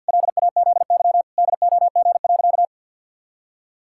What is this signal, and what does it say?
Morse code, heard as a keyed tone, about 52 wpm, 700 Hz. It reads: LAZY DOG.
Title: Pangram Challenge v2 - 50wpm